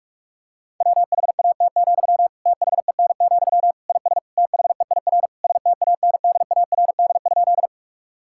Morse code, heard as a keyed tone, about 46 wpm, 700 Hz. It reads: WHAT, THEN, IS THEIR STANDARD?
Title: Sentences from Top 1000 Words - Review Part 6 - 45wpm